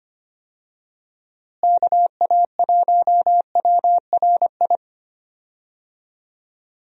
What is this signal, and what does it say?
Morse code, heard as a keyed tone, about 25 wpm, 700 Hz. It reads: KA1WRI
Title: USA Call Signs - 6 Characters - Repeated 2x - 25wpm